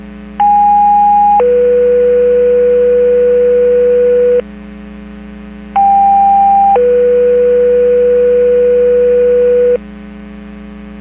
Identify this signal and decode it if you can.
Two-tone page, heard as a one-second tone, then a three-second tone, about 820 then 490 Hz, about 790 then 490 Hz. It is heard from 0.4 to 9.8 s.
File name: Portage 154.385 Paging